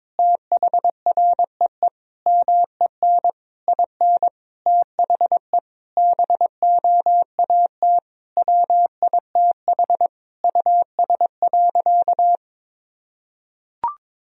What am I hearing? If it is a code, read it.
Morse code, heard as a keyed tone, about 22 wpm, 700 Hz. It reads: THREE MEN IN THE BOAT WITH US.
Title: Sentences from Top 1000 Words - Review Part 3 - 22wpm